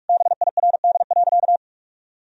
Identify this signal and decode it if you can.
Morse code, heard as a keyed tone, about 46 wpm, 700 Hz. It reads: BIRD.
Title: Sentences from Top 1000 Words - Review Part 2 - 45wpm